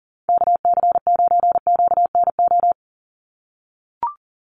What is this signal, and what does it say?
Morse code, heard as a keyed tone, about 40 wpm, 700 Hz. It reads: KC9QNO
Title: USA Call Signs 40wpm